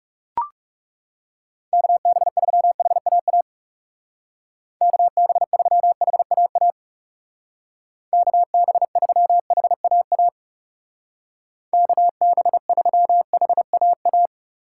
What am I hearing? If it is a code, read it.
Morse code, heard as a keyed tone, about 37 wpm, 700 Hz. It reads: KB3HAA KB3HAA KB3HAA KB3HAA
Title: Speed Racing - USA 2x3 Call Signs 30wpm